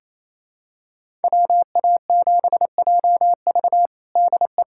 Morse code, heard as a keyed tone, about 28 wpm, 700 Hz. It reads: WA7JV DE
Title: USA Callsign DE Callsign - 5 Characters - 28wpm